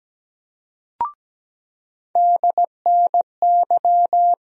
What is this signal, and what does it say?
Morse code, heard as a keyed tone, about 17 wpm, 700 Hz. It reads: DNY
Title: USA Call Sign Postfixes - 3 Letters 17wpm